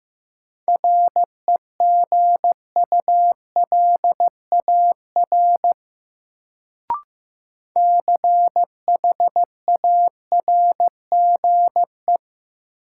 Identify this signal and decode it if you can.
Morse code, heard as a keyed tone, about 15 wpm, 700 Hz. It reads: REGULAR CHARGE
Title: Sentences from Top 900 Words - New Word Drill - 15wpm